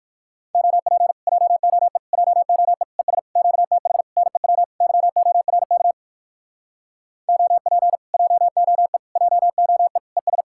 Morse code, heard as a keyed tone, about 63 wpm, 700 Hz. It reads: OP JOE JOE ES QTH NEW YORK OP JOE JOE ES